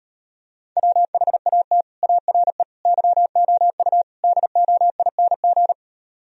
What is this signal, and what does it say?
Morse code, heard as a keyed tone, about 38 wpm, 700 Hz. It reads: WHAT ARE YOU DOING